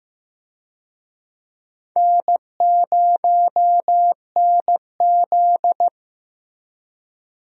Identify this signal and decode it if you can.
Morse code, heard as a keyed tone, about 15 wpm, 700 Hz. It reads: N0NZ
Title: USA Call Signs - 4 Characters - Repeated 2x - 15wpm